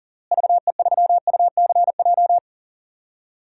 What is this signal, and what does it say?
Morse code, heard as a keyed tone, about 40 wpm, 700 Hz. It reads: VE3UCJ
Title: DXCC-100 Call Signs - 6 Characters - Repeated 3x - 40wpm